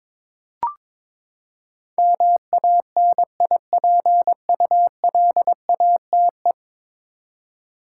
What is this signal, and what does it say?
Morse code, heard as a keyed tone, about 22 wpm, 700 Hz. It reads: MANIPULATE
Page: morse-code-ninja-practice-22wpm